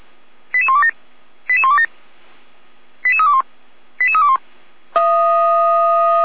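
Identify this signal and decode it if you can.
Five-tone selective call (ZVEI), heard as quick runs of five tones, tones of about 70 ms. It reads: 88127 88127 88321 88321
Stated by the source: ZVEI five-tone selective call received by SDR.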